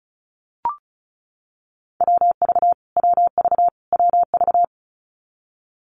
Morse code, heard as a keyed tone, about 34 wpm, 700 Hz. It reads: WV WV WV
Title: US States Repeated 3x 35wpm